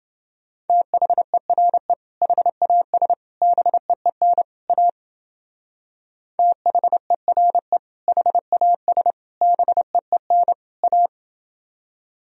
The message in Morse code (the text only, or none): THERE HAS BEEN A THERE HAS BEEN A